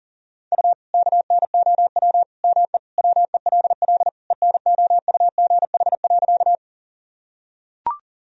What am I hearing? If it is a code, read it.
Morse code, heard as a keyed tone, about 40 wpm, 700 Hz. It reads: U KNOW ME WELL ENOUGH.
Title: Sentences from Top 400 Words and Abbreviations 40wpm